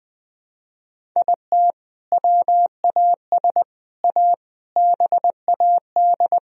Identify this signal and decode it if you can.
Morse code, heard as a keyed tone, about 20 wpm, 700 Hz. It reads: IT WAS A BAD
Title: Sentences from Top 1000 Words - Review Part 1 - 20wpm